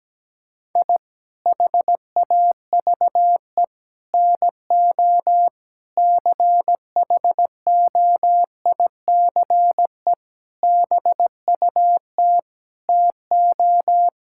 Morse code, heard as a keyed tone, about 17 wpm, 700 Hz. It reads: I HAVE NO CHOICE BUT TO
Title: Sentences from Top 1000 Words - Review Part 6 - 17wpm